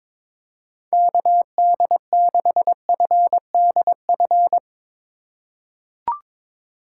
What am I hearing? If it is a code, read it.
Morse code, heard as a keyed tone, about 23 wpm, 700 Hz. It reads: KD6FDF
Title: USA Call Signs - 6 Characters - 2x Character Spacing - 22wpm